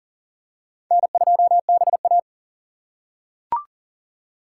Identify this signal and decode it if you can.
Morse code, heard as a keyed tone, about 40 wpm, 700 Hz. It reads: N2BA
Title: USA Call Signs - 3x Character Spacing - 40wpm